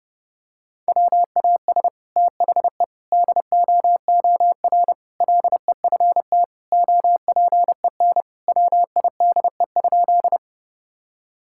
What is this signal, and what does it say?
Morse code, heard as a keyed tone, about 30 wpm, 700 Hz. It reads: WAS THE DOOR LEFT OPEN WIDE?
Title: Sentences from Top 1000 Words - Review Part 6 - 30wpm